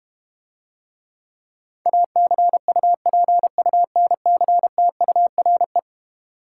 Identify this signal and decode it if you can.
Morse code, heard as a keyed tone, about 32 wpm, 700 Hz. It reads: ACUPUNCTURE